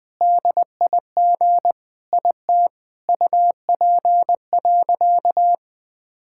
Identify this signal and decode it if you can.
Morse code, heard as a keyed tone, about 21 wpm, 700 Hz. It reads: DIG IT UP.
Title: Sentences from Top 1600 Words 20wpm